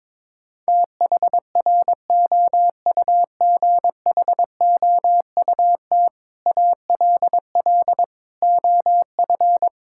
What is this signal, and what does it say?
Morse code, heard as a keyed tone, about 22 wpm, 700 Hz. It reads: THROUGHOUT ALL OF
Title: Sentences from Top 1000 Words - Review Part 5 - 22wpm